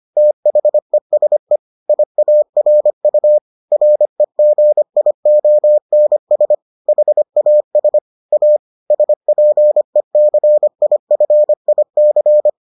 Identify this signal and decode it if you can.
Morse code, heard as a keyed tone, about 25 wpm, 600 Hz. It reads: THESE IARU REGIONS HAS A SPECIFIC